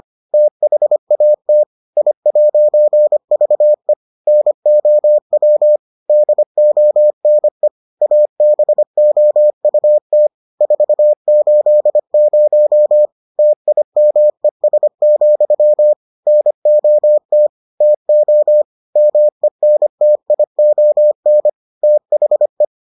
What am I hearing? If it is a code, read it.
Morse code, heard as a keyed tone, about 25 wpm, 600 Hz. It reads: THAT I'VE NOW DONE ABOUT 480 TIMES, NOT TO MENTION THE